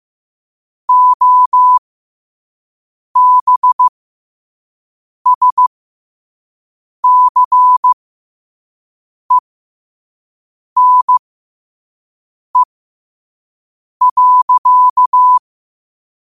Morse code, heard as a keyed tone, about 15 wpm, spaced as for 5 wpm, 1000 Hz. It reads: OBSCENE.
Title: 5 WPM morse code quotes for Fri, 01 Mar 2024